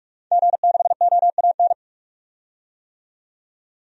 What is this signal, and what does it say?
Morse code, heard as a keyed tone, about 41 wpm, 700 Hz. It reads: G6OAN
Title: DXCC-100 Call Signs - 5 Characters - 45wpm